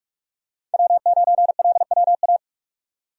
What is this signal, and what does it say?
Morse code, heard as a keyed tone, about 46 wpm, 700 Hz. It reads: W9LWA
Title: USA Call Signs - 4x Character Spacing - 45wpm